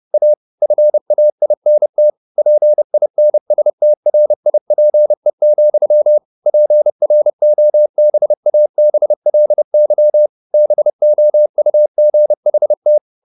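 Morse code, heard as a keyed tone, about 30 wpm, 600 Hz. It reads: A FAINT PINSTRIPE, PROBABLY BOUGHT
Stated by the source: Morse code at 30 wpm, the novel Never by Ken Follett in English, from CW med Gnister.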